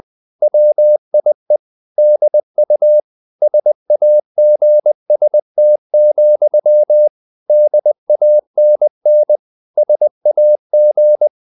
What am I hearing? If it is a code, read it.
Morse code, heard as a keyed tone, about 20 wpm, 600 Hz. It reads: WIE DU SAGST, DANN SAG
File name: dk5ke / Humor-100BpM